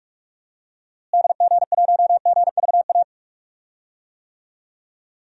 Morse code, heard as a keyed tone, about 44 wpm, 700 Hz. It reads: DG1GVA
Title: DXCC-100 Call Signs - 6 Characters - Repeated 2x - 45wpm